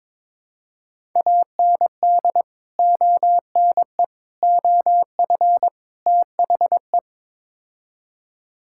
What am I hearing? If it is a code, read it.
Morse code, heard as a keyed tone, about 22 wpm, 700 Hz. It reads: AND ONE OF THE
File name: practice / 22wpm / Build A Phrase - 5-Word Phrases - 22wpm